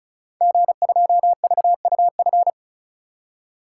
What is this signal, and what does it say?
Morse code, heard as a keyed tone, about 36 wpm, 690 Hz. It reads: G2VUF
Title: Code Groups - 5 Letter:Numbers - 35wpm